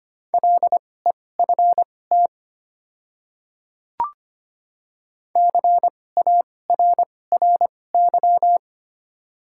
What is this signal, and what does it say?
Morse code, heard as a keyed tone, about 25 wpm, spaced as for 18 wpm, 700 Hz. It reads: LEFT CARRY
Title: 2x Character Spacing - Top 300 Words - 25wpm